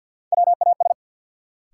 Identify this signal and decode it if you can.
Morse code, heard as a keyed tone, about 48 wpm, 700 Hz. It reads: WAS
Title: Sentences from Top 100 Words - New Word Drill - 50wpm